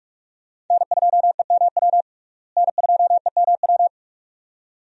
Morse code, heard as a keyed tone, about 41 wpm, 700 Hz. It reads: N2EMW N2EMW
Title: USA Call Signs - 5 Characters - Repeated 2x - 45wpm